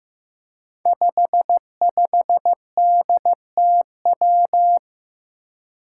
Morse code, heard as a keyed tone, about 15 wpm, 700 Hz. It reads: 55DTW
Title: Five Challenging Letter-Number Combinations 15wpm